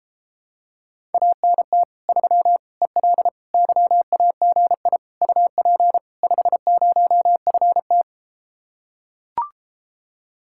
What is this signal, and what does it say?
Morse code, heard as a keyed tone, about 33 wpm, 700 Hz. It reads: ANT 3 EL YAGI UP 50FT